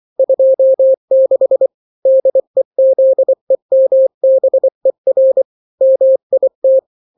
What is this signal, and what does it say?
Morse code, heard as a keyed tone, about 24 wpm, 530 Hz. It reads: 26 DEZEMBER MIT
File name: daily / 23wpm